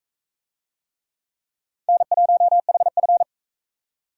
Morse code, heard as a keyed tone, about 40 wpm, 700 Hz. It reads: N1HF